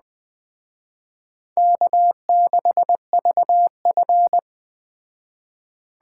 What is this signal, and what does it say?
Morse code, heard as a keyed tone, about 20 wpm, 700 Hz. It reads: K6VF
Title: Speed Racing - USA 1x2 Call Signs 17wpm